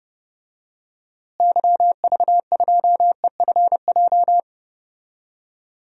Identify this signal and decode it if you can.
Morse code, heard as a keyed tone, about 32 wpm, 700 Hz. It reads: YV2EFJ
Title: DXCC-100 Call Signs - 6 Characters - Repeated 2x - 30wpm